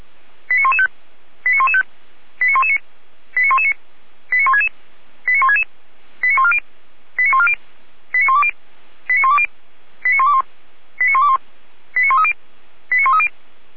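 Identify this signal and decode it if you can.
Five-tone selective call (ZVEI), heard as quick runs of five tones, tones of about 70 ms. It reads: 89115 89115 89119 89119 89166 89166 89140 89140 89120 89120 89121 89121 89130 89130
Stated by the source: received on an SDR; ZVEI five-tone call 89078, 455 directory.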